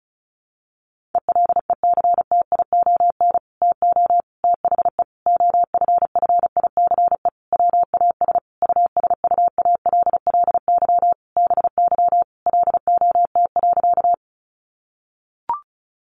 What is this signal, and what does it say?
Morse code, heard as a keyed tone, about 34 wpm, 700 Hz. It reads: ELECTION TO THE OFFICE WAS USUALLY BY LOT.
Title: Sentences from Top 1500 Words 35wpm